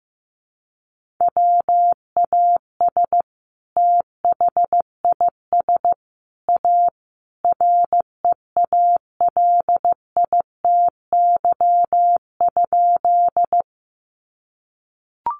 Morse code, heard as a keyed tone, about 15 wpm, 700 Hz. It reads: WAS THIS A REALITY?
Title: Sentences from Top 1200 Words 15wpm